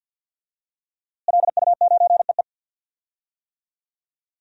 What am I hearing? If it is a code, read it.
Morse code, heard as a keyed tone, about 47 wpm, 700 Hz. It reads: LU9EE
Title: DXCC-100 Call Signs - 5 Characters - Repeated 2x - 50wpm